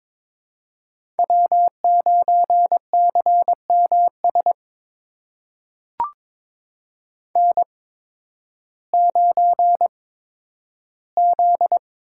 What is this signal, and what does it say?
Morse code, heard as a keyed tone, about 21 wpm, spaced as for 12 wpm, 700 Hz. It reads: W9CMS N9Z
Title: USA Call Signs - 5 Characters - 8x Character Spacing - 22wpm